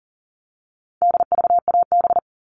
Morse code, heard as a keyed tone, about 40 wpm, 700 Hz. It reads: DVAB
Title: All Cut Numbers - 4-Digit 40wpm